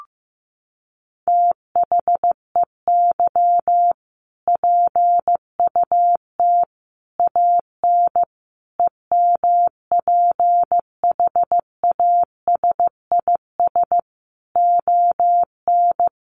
Morse code, heard as a keyed tone, about 14 wpm, 700 Hz. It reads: THEY PUT AN EMPHASIS ON
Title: Sentences from Top 1400 Words 15wpm